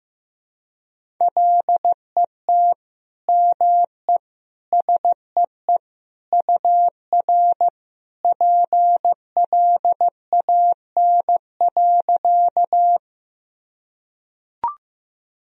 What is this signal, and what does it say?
Morse code, heard as a keyed tone, about 15 wpm, 700 Hz. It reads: LET ME SEE UR PLAN.
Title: Sentences from Top 500 Words and Abbreviations 15wpm